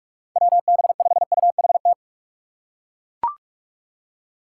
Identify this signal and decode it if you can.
Morse code, heard as a keyed tone, about 42 wpm, 700 Hz. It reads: WB5UHT
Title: USA Call Signs - 6 Characters - 6x Character Spacing - 45wpm